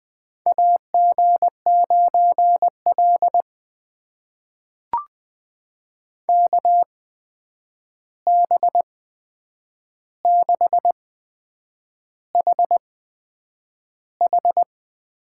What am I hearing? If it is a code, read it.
Morse code, heard as a keyed tone, about 20 wpm, spaced as for 8 wpm, 700 Hz. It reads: AG9L KB6HH
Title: USA Call Signs - 8x Character Spacing - 20wpm